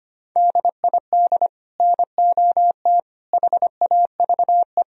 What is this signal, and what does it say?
Morse code, heard as a keyed tone, about 33 wpm, 700 Hz. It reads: DID NOT HAVE